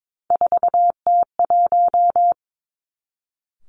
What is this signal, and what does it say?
Morse code, heard as a keyed tone, about 22 wpm, 700 Hz. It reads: 4T1
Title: Three Challenging Letter-Number Combinations 22wpm